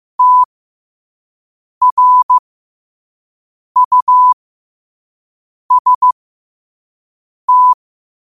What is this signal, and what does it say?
Morse code, heard as a keyed tone, about 15 wpm, spaced as for 5 wpm, 1000 Hz. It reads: TRUST
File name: qotdfiles / 5wpm